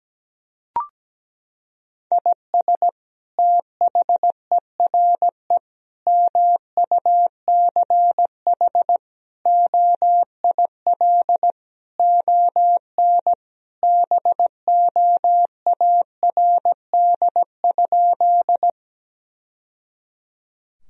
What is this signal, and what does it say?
Morse code, heard as a keyed tone, about 17 wpm, 700 Hz. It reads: IS THERE MUCH OIL ON BOARD?
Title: Sentences from Top 1000 Words - Review Part 4 - 17wpm